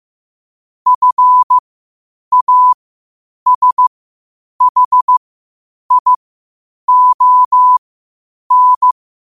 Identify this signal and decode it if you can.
Morse code, heard as a keyed tone, about 15 wpm, spaced as for 8 wpm, 1000 Hz. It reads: FASHION
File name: qotdfiles / 8wpm